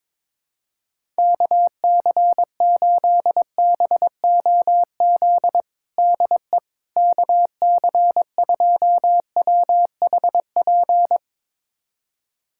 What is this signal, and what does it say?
Morse code, heard as a keyed tone, about 21 wpm, 700 Hz. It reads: KC8BOZ DE KC2WHP
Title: USA Callsign DE Callsign - 6 Characters - 22wpm